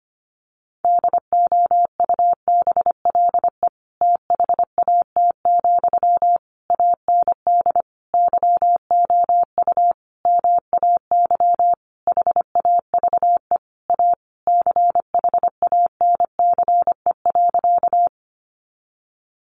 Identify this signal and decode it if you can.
Morse code, heard as a keyed tone, about 25 wpm, 700 Hz. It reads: DOUBLE THAT, AND YOU MAY HAVE A CHANCE.
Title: Sentences from Top 1400 Words 25wpm